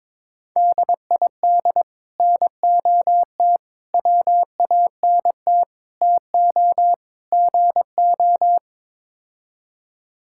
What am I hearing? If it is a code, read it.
Morse code, heard as a keyed tone, about 22 wpm, 700 Hz. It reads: DID NOT WANT TO GO
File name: Build A Phrase - 5-Word Phrases - 22wpm